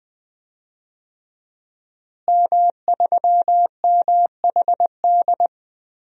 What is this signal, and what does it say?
Morse code, heard as a keyed tone, about 20 wpm, 700 Hz. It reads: M3MHD
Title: DXCC-100 Call Signs - 5 Characters - Repeated 3x - 20wpm